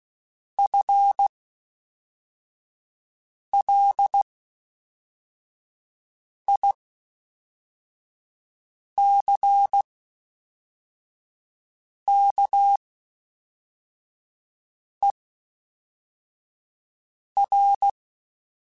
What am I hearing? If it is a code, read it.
Morse code, heard as a keyed tone, about 16 wpm, spaced as for 4 wpm, 800 Hz. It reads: FLICKER
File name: Spaced / FL Runs sp mp3s / FL Runs sp @5